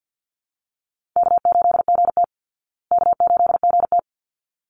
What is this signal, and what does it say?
Morse code, heard as a keyed tone, about 48 wpm, 700 Hz. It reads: K8GT K8GT
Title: Easy Call Signs Repeated 2x - 50wpm